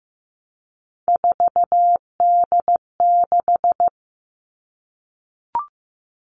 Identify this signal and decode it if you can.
Morse code, heard as a keyed tone, about 15 wpm, 700 Hz. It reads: 4D6
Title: All Cut Numbers - 3-Digit 15wpm